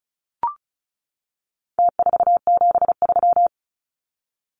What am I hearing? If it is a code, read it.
Morse code, heard as a keyed tone, about 34 wpm, 700 Hz. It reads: T473
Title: Numbers - Contest Style - 4-Digit 35wpm